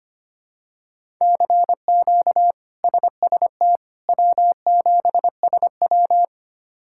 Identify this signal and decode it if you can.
Morse code, heard as a keyed tone, about 25 wpm, 700 Hz. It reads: CQ SST W7SW
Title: K1USN SST QSO Practice - 25wpm